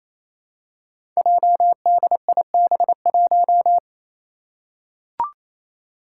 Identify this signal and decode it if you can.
Morse code, heard as a keyed tone, about 27 wpm, 700 Hz. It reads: JDIB1
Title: Five Challenging Letter-Number Combinations 28wpm